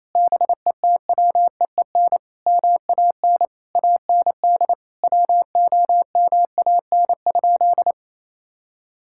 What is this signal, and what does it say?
Morse code, heard as a keyed tone, about 28 wpm, 700 Hz. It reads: BETWEEN MAN AND WOMAN?
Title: Sentences from Top 1600 Words 28wpm